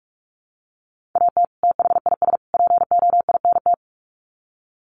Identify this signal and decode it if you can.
Morse code, heard as a keyed tone, about 44 wpm, 700 Hz. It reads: AT THIS POINT
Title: Sets of 3 Words 45wpm